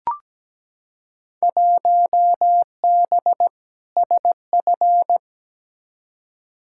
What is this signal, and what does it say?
Morse code, heard as a keyed tone, about 17 wpm, 700 Hz. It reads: 1B SF
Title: Field Day Exchanges Encore - 17wpm